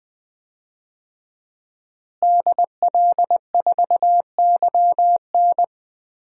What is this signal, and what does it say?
Morse code, heard as a keyed tone, about 20 wpm, 700 Hz. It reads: DL4YN
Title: DXCC-100 Call Signs - 5 Characters - Repeated 2x - 20wpm